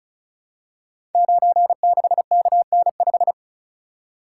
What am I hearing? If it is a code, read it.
Morse code, heard as a keyed tone, about 36 wpm, 700 Hz. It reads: 96KN5
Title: Code Groups - 5 Letter:Numbers - 35wpm